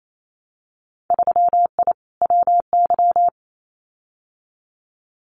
Morse code, heard as a keyed tone, about 28 wpm, 700 Hz. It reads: 3I WY
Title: Winter Field Day Exchanges Encore 28wpm